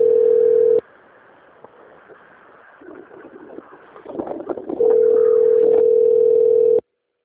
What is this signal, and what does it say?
Ringback tone (North America), 440 + 480 Hz.
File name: out- - - - - .6999